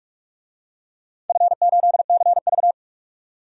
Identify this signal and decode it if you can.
Morse code, heard as a keyed tone, about 42 wpm, 700 Hz. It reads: F8CV
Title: DXCC-100 Call Signs - 4 Characters - 45wpm